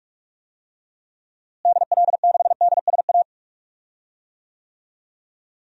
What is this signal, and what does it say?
Morse code, heard as a keyed tone, about 46 wpm, 700 Hz. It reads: DL6DSA
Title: DXCC-100 Call Signs - 6 Characters - Repeated 2x - 45wpm